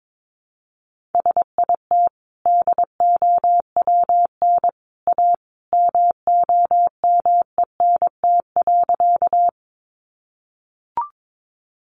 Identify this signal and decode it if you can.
Morse code, heard as a keyed tone, about 22 wpm, 700 Hz. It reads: SIT DOWN A MOMENT.
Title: Sentences from Top 1000 Words 22wpm